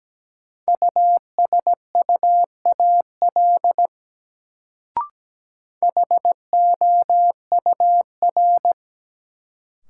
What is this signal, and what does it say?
Morse code, heard as a keyed tone, about 17 wpm, 700 Hz. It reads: USUAL HOUR
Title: Sentences from Top 400 Words - New Word Drill - 17wpm